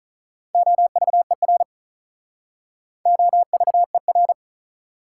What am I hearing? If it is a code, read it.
Morse code, heard as a keyed tone, about 48 wpm, 700 Hz. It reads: OVER OVER